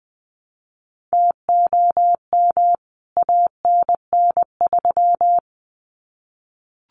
Morse code, heard as a keyed tone, about 20 wpm, 700 Hz. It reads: TOM ANN3
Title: CWT Exchange 20wpm